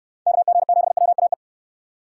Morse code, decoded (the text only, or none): KG7WAE